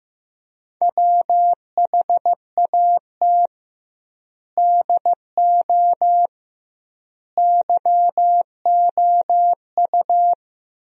Morse code, heard as a keyed tone, about 15 wpm, 700 Hz. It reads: WHAT DO YOU
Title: 2x Word-Spacing - Sentences from Top 300 Words - 15wpm